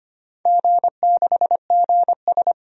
Morse code, heard as a keyed tone, about 25 wpm, 700 Hz. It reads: G6GS
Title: DXCC-100 Call Signs - 4 Characters - 25wpm